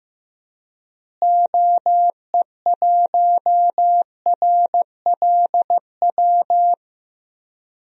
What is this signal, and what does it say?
Morse code, heard as a keyed tone, about 15 wpm, 700 Hz. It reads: OE1RLW
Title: DXCC-100 Call Signs - 6 Characters - Repeated 3x - 15wpm